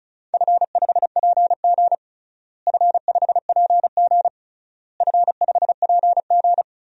Morse code, heard as a keyed tone, about 36 wpm, 700 Hz. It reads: F5PG F5PG F5PG
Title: DXCC-100 Call Signs - 4 Characters - Repeated 3x - 35wpm